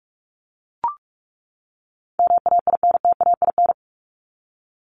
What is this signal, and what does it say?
Morse code, heard as a keyed tone, about 44 wpm, 700 Hz. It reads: MAINTAIN
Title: Top 501-750 Words 45wpm